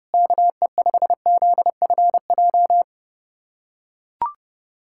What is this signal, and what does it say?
Morse code, heard as a keyed tone, about 30 wpm, 700 Hz. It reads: KE5ZFJ
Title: USA Callsign DE Callsign 30wpm